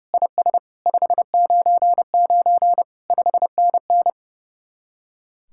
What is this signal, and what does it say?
Morse code, heard as a keyed tone, about 34 wpm, 700 Hz. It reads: IS 599 5NN
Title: Speed-Racing Common QSO Phrases - 30wpm